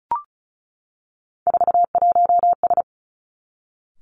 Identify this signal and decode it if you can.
Morse code, heard as a keyed tone, about 36 wpm, 700 Hz. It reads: 41S
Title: Three Challenging Letter-Number Combinations 35wpm